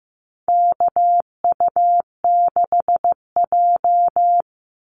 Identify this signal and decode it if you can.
Morse code, heard as a keyed tone, about 15 wpm, 700 Hz. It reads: KU6J
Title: NA SOTA Chaser Callsigns 15wpm